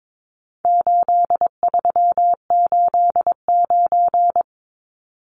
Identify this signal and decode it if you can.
Morse code, heard as a keyed tone, about 22 wpm, 700 Hz. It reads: 8389
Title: Numbers - 4-Digit 22wpm